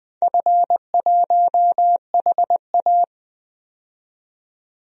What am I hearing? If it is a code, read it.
Morse code, heard as a keyed tone, about 20 wpm, 700 Hz. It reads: F1HA
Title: DXCC-100 Call Signs - 4 Characters - 20wpm